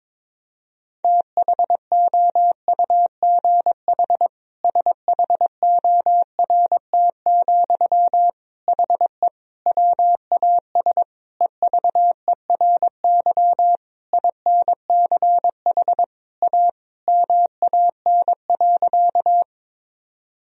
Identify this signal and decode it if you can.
Morse code, heard as a keyed tone, about 21 wpm, 700 Hz. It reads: THOUGH SHORT, HE WAS EVERY INCH A MAN.
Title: Sentences from Top 1000 Words - Review Part 3 - 22wpm